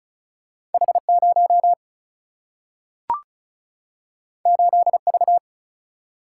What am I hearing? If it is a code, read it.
Morse code, heard as a keyed tone, about 36 wpm, 700 Hz. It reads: H 0 8 V
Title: Single-Letter Single-Number 35wpm